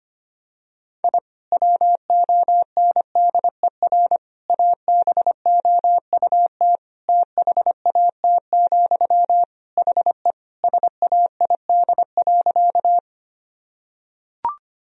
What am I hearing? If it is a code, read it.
Morse code, heard as a keyed tone, about 25 wpm, 700 Hz. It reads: I WONDER ABOUT THAT, HE SAID.
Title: Sentences from Top 1000 Words - Review Part 3 - 25wpm